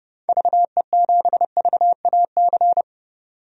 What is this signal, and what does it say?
Morse code, heard as a keyed tone, about 30 wpm, 700 Hz. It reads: VE7VAC
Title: DXCC-100 Call Signs - 6 Characters - 30wpm